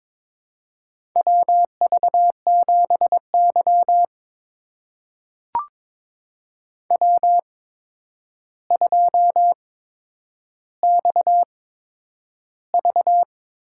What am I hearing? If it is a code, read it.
Morse code, heard as a keyed tone, about 22 wpm, spaced as for 9 wpm, 700 Hz. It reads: WV7Y W2XV
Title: USA Call Signs - 4 Characters - 8x Character Spacing - 22wpm